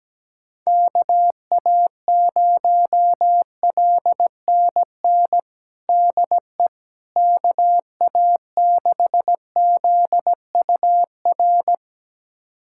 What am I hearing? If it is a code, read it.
Morse code, heard as a keyed tone, about 17 wpm, 700 Hz. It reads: KA0LNN DE KA6ZUR
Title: USA Callsign DE Callsign - 6 Characters - 17wpm